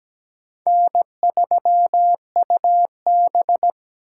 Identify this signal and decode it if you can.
Morse code, heard as a keyed tone, about 17 wpm, 700 Hz. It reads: N3UB
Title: USA Call Signs - 4 Characters - 3x Character Spacing - 17wpm